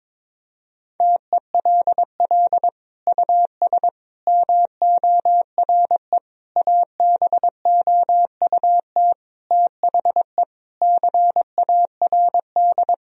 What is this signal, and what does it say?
Morse code, heard as a keyed tone, about 22 wpm, 700 Hz. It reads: TELL US MORE ABOUT THE CARD
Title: Sentences from Top 1000 Words - Review Part 4 - 22wpm